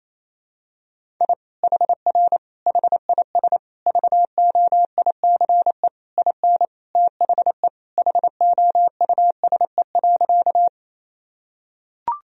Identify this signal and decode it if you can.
Morse code, heard as a keyed tone, about 28 wpm, 700 Hz. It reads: I HR HIS VOICE IN THE HOUSE.
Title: Sentences from Top 500 Words and Abbreviations 28wpm